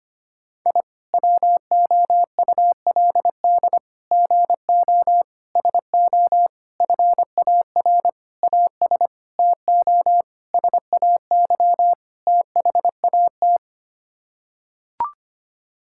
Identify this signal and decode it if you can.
Morse code, heard as a keyed tone, about 25 wpm, 700 Hz. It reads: I WOULD GO SO FAR AS TO SAY THAT
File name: Sets of 9 Words - 25wpm